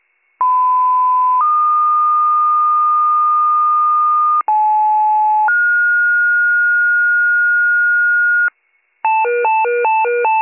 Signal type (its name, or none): two-tone page